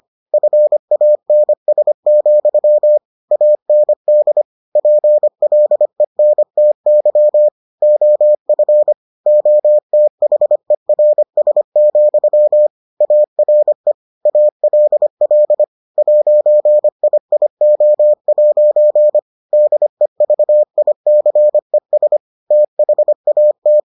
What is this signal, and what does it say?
Morse code, heard as a keyed tone, about 25 wpm, 600 Hz. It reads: FANS, AND PLENTY OF OTHERS, ARE ALL 'IIO' DEVICES THAT